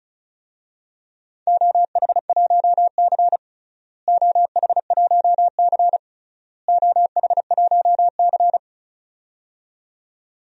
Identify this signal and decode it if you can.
Morse code, heard as a keyed tone, about 36 wpm, 700 Hz. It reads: OH1C OH1C OH1C
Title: DXCC-100 Call Signs - 4 Characters - Repeated 3x - 35wpm